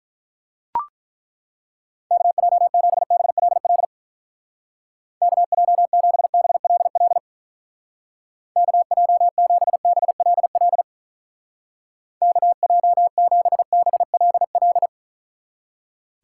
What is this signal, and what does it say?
Morse code, heard as a keyed tone, about 43 wpm, 700 Hz. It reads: KJ7BLL KJ7BLL KJ7BLL KJ7BLL
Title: Speed Racing - USA 2x3 Call Signs 35wpm